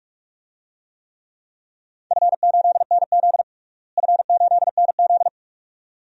Morse code, heard as a keyed tone, about 46 wpm, 700 Hz. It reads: F8NZ F8NZ
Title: DXCC-100 Call Signs - 4 Characters - Repeated 2x - 45wpm